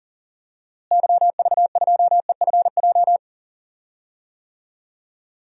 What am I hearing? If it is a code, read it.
Morse code, heard as a keyed tone, about 40 wpm, 700 Hz. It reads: YV2EFJ
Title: DXCC-100 Call Signs - 6 Characters - Repeated 2x - 40wpm